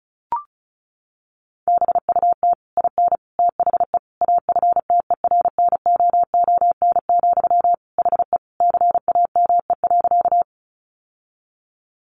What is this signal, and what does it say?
Morse code, heard as a keyed tone, about 34 wpm, 700 Hz. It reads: BUT IN THE AFTERNOON, HE CAME.
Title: Sentences from Top 1500 Words 35wpm